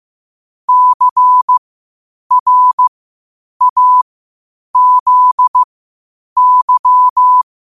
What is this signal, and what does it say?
Morse code, heard as a keyed tone, about 15 wpm, spaced as for 8 wpm, 1000 Hz. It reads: CRAZY
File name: qotdfiles / 8wpm